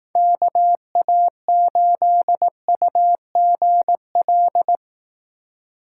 Morse code, heard as a keyed tone, about 18 wpm, 700 Hz. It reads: KA8UGL
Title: Speed Racing - USA 2x3 Call Signs 15wpm